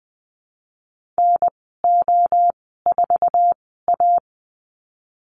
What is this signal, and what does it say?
Morse code, heard as a keyed tone, about 20 wpm, spaced as for 15 wpm, 700 Hz. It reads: NO4A
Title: 069 Lesson 12g - Letter d - Call Signs 20wpm